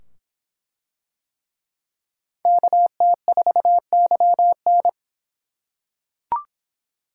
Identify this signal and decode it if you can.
Morse code, heard as a keyed tone, about 24 wpm, 700 Hz. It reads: KT4YN